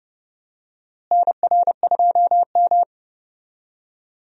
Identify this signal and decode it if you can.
Morse code, heard as a keyed tone, about 30 wpm, 700 Hz. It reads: NR2M
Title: USA Call Signs - 4 Characters - Repeated 3x - 30wpm